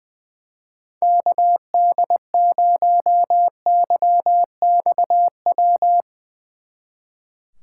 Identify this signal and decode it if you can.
Morse code, heard as a keyed tone, about 22 wpm, 700 Hz. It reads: KD0YXW
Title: Speed Racing - USA 2x3 Call Signs 20wpm